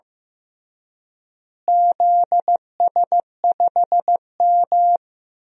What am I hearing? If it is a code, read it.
Morse code, heard as a keyed tone, about 15 wpm, 700 Hz. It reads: ZS5M
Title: DXCC-100 Call Signs - 4 Characters - Repeated 2x - 15wpm